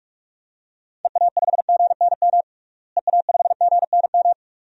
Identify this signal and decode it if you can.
Morse code, heard as a keyed tone, about 46 wpm, 700 Hz. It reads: EA5GNM EA5GNM
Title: DXCC-100 Call Signs - 6 Characters - Repeated 2x - 45wpm